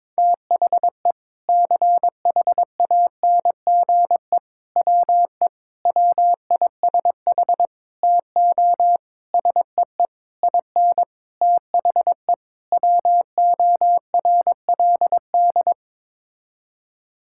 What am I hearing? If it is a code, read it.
Morse code, heard as a keyed tone, about 22 wpm, 700 Hz. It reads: THE CHANGE WE WISH TO SEE IN THE WORLD
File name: Sets of 9 Words - 22wpm